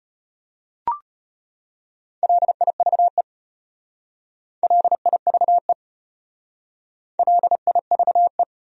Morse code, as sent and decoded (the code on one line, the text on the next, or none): .-.. .. ...- . / .-.. .. ...- . / .-.. .. ...- .
LIVE LIVE LIVE